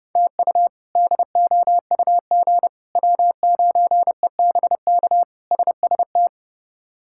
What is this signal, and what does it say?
Morse code, heard as a keyed tone, about 30 wpm, 700 Hz. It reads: TU DOUG W9EBK SST
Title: K1USN SST QSO Practice - 30wpm